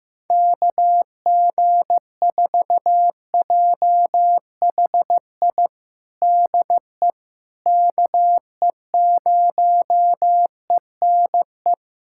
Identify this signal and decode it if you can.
Morse code, heard as a keyed tone, about 16 wpm, 700 Hz. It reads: KG4JHI DE KE0ENE
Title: USA Callsign DE Callsign 15wpm